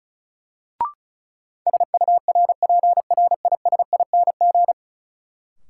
Morse code, heard as a keyed tone, about 34 wpm, 700 Hz. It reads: SURPRISING
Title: Sets of 5 Words - New Word Drill - 35wpm